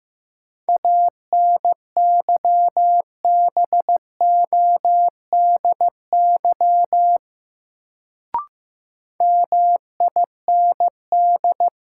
Morse code, heard as a keyed tone, about 15 wpm, 700 Hz. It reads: ANYBODY MIND
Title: Sets of 5 Words - New Word Drill - 15wpm